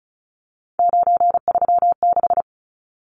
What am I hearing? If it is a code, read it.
Morse code, heard as a keyed tone, about 34 wpm, 700 Hz. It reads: 936
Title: Numbers - 3-Digit 35wpm